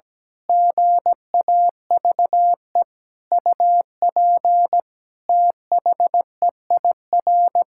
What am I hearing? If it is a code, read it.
Morse code, heard as a keyed tone, about 17 wpm, 700 Hz. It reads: GAVE UP THEIR
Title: Pangram Challenge v2 - 17wpm